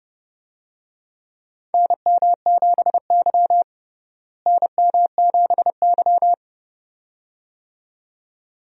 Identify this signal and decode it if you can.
Morse code, heard as a keyed tone, about 30 wpm, 700 Hz. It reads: NM7Y NM7Y
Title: USA Call Signs - 4 Characters - Repeated 2x - 30wpm